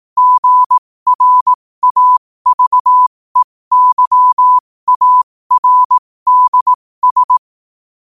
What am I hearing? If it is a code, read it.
Morse code, heard as a keyed tone, about 18 wpm, spaced as for 15 wpm, 1000 Hz. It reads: GRAVEYARDS
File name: qotdfiles / 15wpm